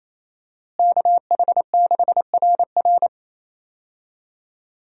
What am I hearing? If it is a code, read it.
Morse code, heard as a keyed tone, about 28 wpm, 700 Hz. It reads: KH6RR
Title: USA Call Signs - 5 Characters - Repeated 2x - 28wpm